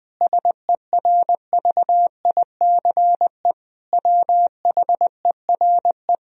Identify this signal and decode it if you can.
Morse code, heard as a keyed tone, about 20 wpm, 700 Hz. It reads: SERVICE WHERE
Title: Sentences from Top 1000 Words - Review Part 4 - 20wpm